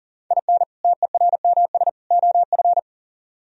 Morse code, heard as a keyed tone, about 51 wpm, 700 Hz. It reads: IN TERMS OF